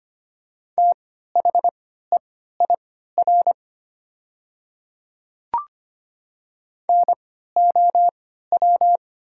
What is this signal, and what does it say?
Morse code, heard as a keyed tone, about 25 wpm, spaced as for 14 wpm, 700 Hz. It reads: THEIR NOW
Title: 3x Character Spacing - Top 100 Words - 25wpm